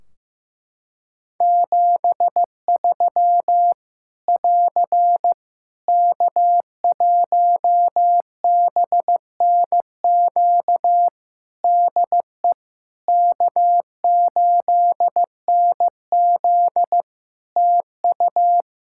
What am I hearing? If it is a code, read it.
Morse code, heard as a keyed tone, about 15 wpm, 700 Hz. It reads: 73 <AR> K1BNQ DE K8NZ TU
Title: Practice QSO - Exchange 4 - Ending - 15wpm